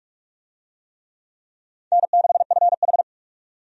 Morse code, heard as a keyed tone, about 46 wpm, 700 Hz. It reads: N6FH
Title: USA Call Signs - 4 Characters - 45wpm